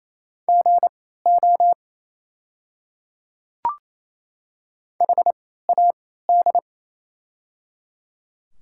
Morse code, heard as a keyed tone, about 28 wpm, spaced as for 16 wpm, 700 Hz. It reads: GO HAD
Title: 3x Character Spacing - Top 100 Words - 28wpm